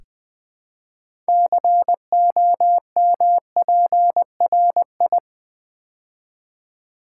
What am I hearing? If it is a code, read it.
Morse code, heard as a keyed tone, about 20 wpm, 700 Hz. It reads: COMPRI
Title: Build A Word - 3 Letters At A Time - 8-Letter Words - 20wpm